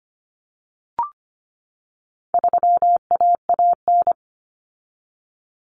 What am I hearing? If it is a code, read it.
Morse code, heard as a keyed tone, about 25 wpm, 700 Hz. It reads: 3AAN
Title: Common Cut Numbers - 4-Digit 25wpm